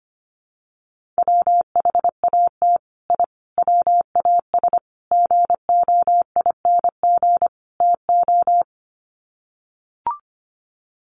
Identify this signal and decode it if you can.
Morse code, heard as a keyed tone, about 25 wpm, 700 Hz. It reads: WHAT I WAS GOING TO
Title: Sets of 5 Words Encore2 25wpm